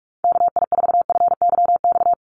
Morse code, heard as a keyed tone, about 44 wpm, 690 Hz. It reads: KI4FYX
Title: USA Call Signs 45wpm